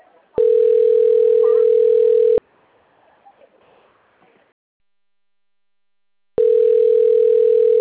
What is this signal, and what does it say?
Ringback tone (North America), 440 + 480 Hz.